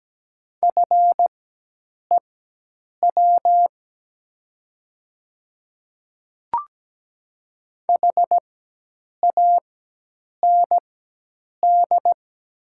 Morse code, heard as a keyed tone, about 17 wpm, spaced as for 8 wpm, 700 Hz. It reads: FEW HAND
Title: 4x Character Spacing - Top 300 Words - 17wpm